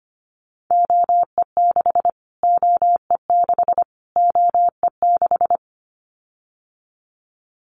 Morse code, heard as a keyed tone, about 25 wpm, 700 Hz. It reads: OE6 OE6 OE6
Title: Callsign Prefixes for Top 25 DXCC - Repeated 3x - 25wpm